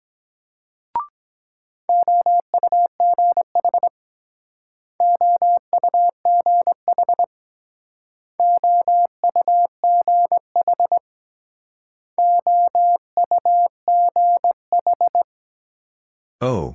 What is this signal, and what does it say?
Morse code, heard as a keyed tone, about 23 wpm, 700 Hz. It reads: OUGH OUGH OUGH OUGH